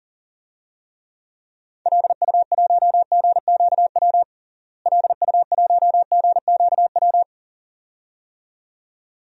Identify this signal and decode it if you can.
Morse code, heard as a keyed tone, about 40 wpm, 700 Hz. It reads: LU1GQW LU1GQW
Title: DXCC-100 Call Signs - 6 Characters - Repeated 2x - 40wpm